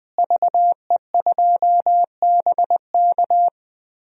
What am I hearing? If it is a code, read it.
Morse code, heard as a keyed tone, about 20 wpm, 700 Hz. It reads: VE2BK
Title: DXCC-100 Call Signs - 5 Characters - Repeated 3x - 20wpm